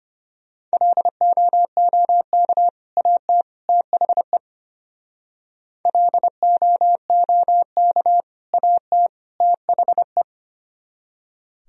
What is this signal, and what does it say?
Morse code, heard as a keyed tone, about 31 wpm, 700 Hz. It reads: LOOK AT THE LOOK AT THE
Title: Speed-Racing - Sets of 3 Words 25wpm